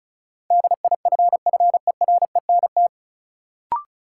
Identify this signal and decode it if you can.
Morse code, heard as a keyed tone, about 36 wpm, 700 Hz. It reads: DIFFERENT
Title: Sentences from Top 600 Words - New Word Drill - 35wpm